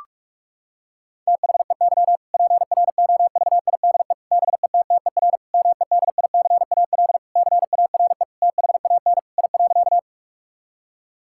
Morse code, heard as a keyed tone, about 46 wpm, 700 Hz. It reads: THEY PROVIDE BETTER MEDICAL CARE THAN I.
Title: Sentences from Top 1000 Words - Review Part 5 - 45wpm